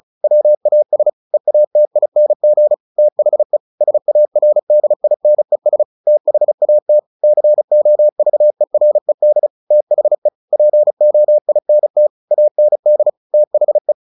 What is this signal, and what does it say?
Morse code, heard as a keyed tone, about 35 wpm, 600 Hz. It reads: WAS EATING THE SARDINES THAT COVERED THE POINT AND THE
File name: the old man and the sea / the old man and the sea 1 35wpm